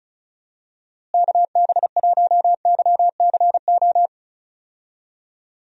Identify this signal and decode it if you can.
Morse code, heard as a keyed tone, about 36 wpm, 700 Hz. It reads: KB1YCO
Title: USA Call Signs - 6 Characters - Repeated 2x - 35wpm